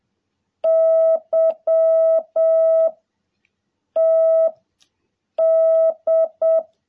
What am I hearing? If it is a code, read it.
Morse code, heard as a keyed tone, about 7 wpm, 660 Hz. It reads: Y T D
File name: jh8cbh / 2021cw